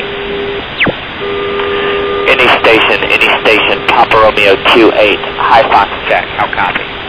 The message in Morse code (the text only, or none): none